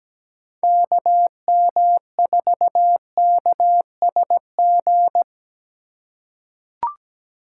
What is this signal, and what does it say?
Morse code, heard as a keyed tone, about 17 wpm, 700 Hz. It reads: KM4KSG
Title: USA Call Signs - 3x Character Spacing - 17wpm